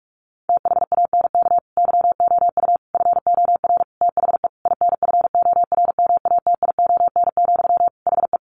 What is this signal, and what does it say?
Morse code, heard as a keyed tone, about 44 wpm, 700 Hz. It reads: THANK YOU FOR THE INFORMATION, HE
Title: Sentences from Top 600 Words 45wpm